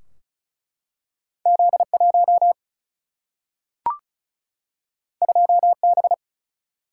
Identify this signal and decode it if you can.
Morse code, heard as a keyed tone, about 36 wpm, 700 Hz. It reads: Z1 2B
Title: Single-Letter Single-Number 35wpm